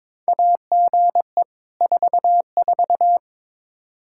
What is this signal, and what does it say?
Morse code, heard as a keyed tone, about 22 wpm, 700 Hz. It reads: AGE 44
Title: Practice QSO - Exchange 3 - More Info - 22wpm